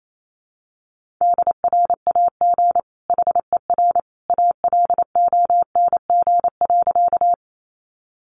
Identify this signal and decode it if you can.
Morse code, heard as a keyed tone, about 28 wpm, 700 Hz. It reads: DRAG HER ALONG.
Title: Sentences from Top 1500 Words 28wpm